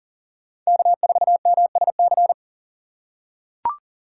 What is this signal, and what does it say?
Morse code, heard as a keyed tone, about 40 wpm, 700 Hz. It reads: K4MSC
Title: USA Call Signs - 5 Characters - 4x Character Spacing - 40wpm